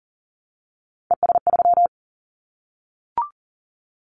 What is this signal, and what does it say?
Morse code, heard as a keyed tone, about 42 wpm, 690 Hz. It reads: ES3
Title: Callsign Prefixes for Top 100 DXCC - 40wpm